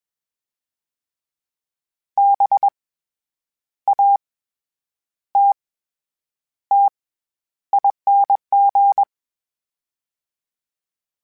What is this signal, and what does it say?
Morse code, heard as a keyed tone, about 21 wpm, 800 Hz. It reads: B A T T ING
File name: Special INGs-mp3s / Special INGs-mp3s / SlowToFaster ing @20